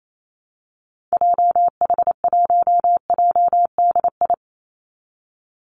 Morse code, heard as a keyed tone, about 28 wpm, 700 Hz. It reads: JH1JDI
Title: Call Signs Repeated 2x - 28wpm